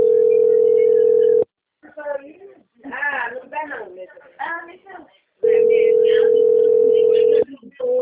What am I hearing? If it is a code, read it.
Ringback tone (North America), 440 + 480 Hz, with speech.
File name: out- - - - - .22465